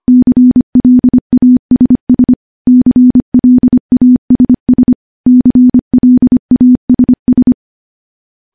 Morse code, heard as a keyed tone, about 25 wpm, 250 Hz. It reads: CLASS CLASS CLASS
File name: morse / morse-training-25WPM